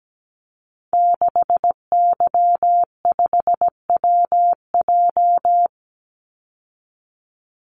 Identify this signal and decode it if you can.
Morse code, heard as a keyed tone, about 17 wpm, 700 Hz. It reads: 6Y5WJ
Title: Call Signs Repeated 3x - 17wpm